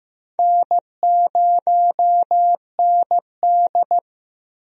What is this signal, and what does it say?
Morse code, heard as a keyed tone, about 16 wpm, 700 Hz. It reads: N0ND
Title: USA Call Signs - 4 Characters - Repeated 3x - 15wpm